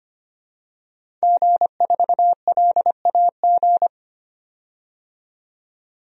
Morse code, heard as a keyed tone, about 25 wpm, 700 Hz. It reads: G4LAG
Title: DXCC-100 Call Signs - 5 Characters - 25wpm